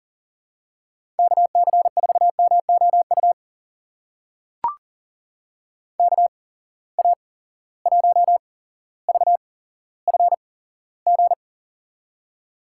Morse code, heard as a keyed tone, about 40 wpm, spaced as for 17 wpm, 700 Hz. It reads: KC4MOU KA1VFG
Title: USA Call Signs - 6 Characters - 8x Character Spacing - 40wpm